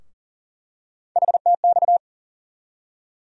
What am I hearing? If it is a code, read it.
Morse code, heard as a keyed tone, about 40 wpm, 700 Hz. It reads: HTX
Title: USA Call Sign Postfixes - 3 Letters 40wpm